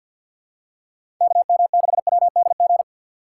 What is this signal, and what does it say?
Morse code, heard as a keyed tone, about 52 wpm, 700 Hz. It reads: KM6WDG
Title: USA Call Signs - 6 Characters - 3x Character Spacing - 50wpm